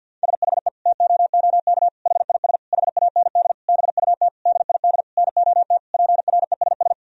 Morse code, heard as a keyed tone, about 52 wpm, 700 Hz. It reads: SHE TOOK HIS HAND BUT DID NOT PRESS
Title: Sentences from Top 1000 Words - Review Part 1 - 50wpm